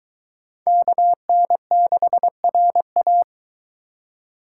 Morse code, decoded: KN6RA

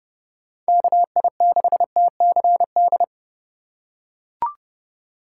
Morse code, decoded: KI6TCD

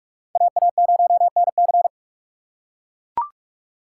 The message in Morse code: AA0NC

